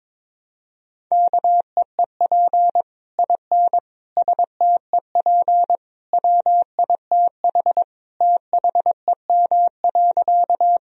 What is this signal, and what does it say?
Morse code, heard as a keyed tone, about 22 wpm, 700 Hz. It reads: KEEP IN STEP WITH THEM.
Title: Sentences from Top 1000 Words - Review Part 2 - 22wpm